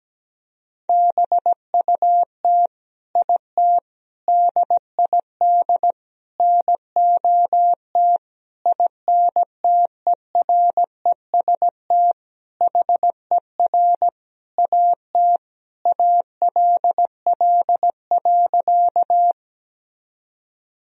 Morse code, heard as a keyed tone, about 17 wpm, 700 Hz. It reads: BUT IT DID NOT INTEREST HER AT ALL.
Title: Sentences from Top 1000 Words - Review Part 2 - 17wpm